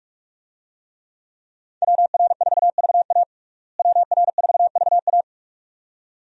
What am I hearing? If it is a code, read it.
Morse code, heard as a keyed tone, about 46 wpm, 700 Hz. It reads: WR4VA WR4VA
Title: USA Call Signs - 5 Characters - Repeated 2x - 45wpm